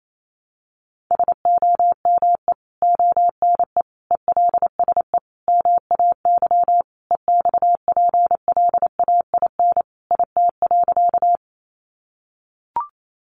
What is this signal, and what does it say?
Morse code, heard as a keyed tone, about 28 wpm, 700 Hz. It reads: SOME ONE ELSE MAY EXPLAIN IT.
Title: Sentences from Top 1100 Words 28wpm